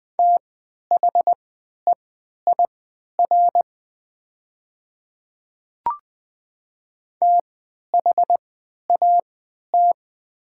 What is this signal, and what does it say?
Morse code, heard as a keyed tone, about 20 wpm, spaced as for 11 wpm, 700 Hz. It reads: THEIR THAT
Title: 3x Character Spacing - Top 100 Words - 20wpm